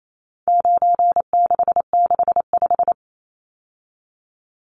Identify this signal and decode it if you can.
Morse code, heard as a keyed tone, about 28 wpm, 700 Hz. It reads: 9665
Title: Numbers - 4-Digit 28wpm